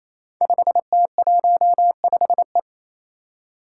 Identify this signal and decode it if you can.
Morse code, heard as a keyed tone, about 28 wpm, 700 Hz. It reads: 5T15E
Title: Five Challenging Letter-Number Combinations 28wpm